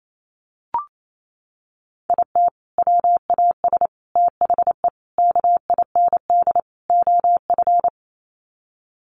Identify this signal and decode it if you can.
Morse code, heard as a keyed tone, about 28 wpm, 700 Hz. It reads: IT WAS THE KIND OF
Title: Sets of 5 Words Encore2 28wpm